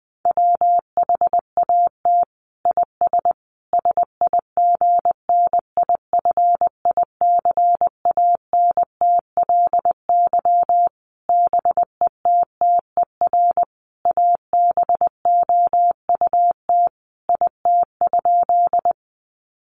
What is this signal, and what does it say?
Morse code, heard as a keyed tone, about 20 wpm, 700 Hz. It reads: WHAT IS SIGNIFICANTLY BETTER ABOUT IT?
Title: Sentences from Top 1500 Words 20wpm